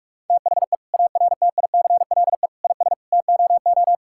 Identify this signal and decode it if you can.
Morse code, heard as a keyed tone, about 46 wpm, 700 Hz. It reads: THE ARTICLE IS TOO
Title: Sentences from Top 1000 Words - Review Part 5 - 45wpm